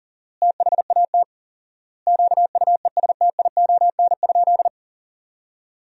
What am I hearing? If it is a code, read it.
Morse code, heard as a keyed tone, about 40 wpm, 700 Hz. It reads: THAT QUESTION?
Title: 4x Word-Spacing - Sentences from Top 400 Words - 40wpm